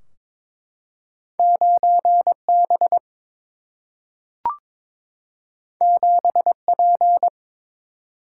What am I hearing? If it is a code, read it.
Morse code, heard as a keyed tone, about 22 wpm, 700 Hz. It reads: 9B 7P
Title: Single-Letter Single-Number 22wpm